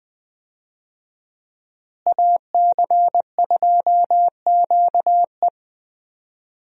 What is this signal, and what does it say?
Morse code, heard as a keyed tone, about 20 wpm, 700 Hz. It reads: AC2QE
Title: USA Call Signs - 5 Characters - Repeated 2x - 20wpm